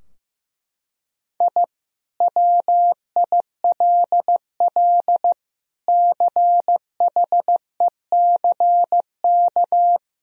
Morse code, heard as a keyed tone, about 16 wpm, 700 Hz. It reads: I WILL CHECK
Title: Sentences from Top 1000 Words - Review Part 3 - 15wpm